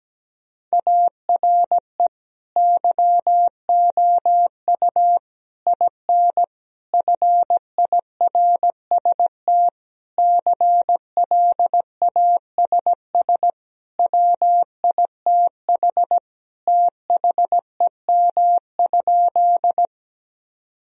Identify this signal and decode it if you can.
Morse code, heard as a keyed tone, about 17 wpm, 700 Hz. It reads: ARE YOU IN FIRST CLASS WITH THEM?
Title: Sentences from Top 1000 Words - Review Part 2 - 17wpm